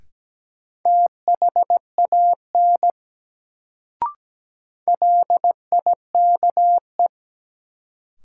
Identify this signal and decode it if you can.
Morse code, heard as a keyed tone, about 17 wpm, 700 Hz. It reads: THAN LIKE
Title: Sentences from Top 100 Words - New Word Drill - 17wpm